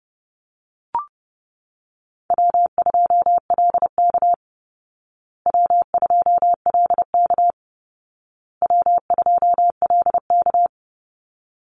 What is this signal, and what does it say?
Morse code, heard as a keyed tone, about 30 wpm, 700 Hz. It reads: W2LK W2LK W2LK
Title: Easy Call Signs Repeated 3x - 30wpm